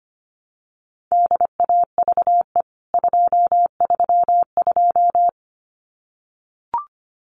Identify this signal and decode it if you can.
Morse code, heard as a keyed tone, about 25 wpm, 690 Hz. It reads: DAVE 232
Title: CWT Exchange 25wpm